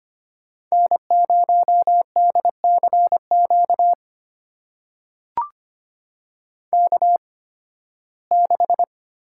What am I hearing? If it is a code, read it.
Morse code, heard as a keyed tone, about 25 wpm, spaced as for 16 wpm, 700 Hz. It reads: N0DCQ K6
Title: USA Call Signs - 5 Characters - 8x Character Spacing - 25wpm